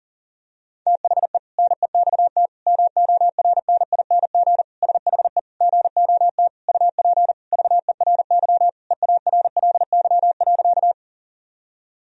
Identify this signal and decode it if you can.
Morse code, heard as a keyed tone, about 40 wpm, 700 Hz. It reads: THE NEXT MORNING SHE GOT UP VERY EARLY.
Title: Sentences from Top 1000 Words - Review Part 2 - 40wpm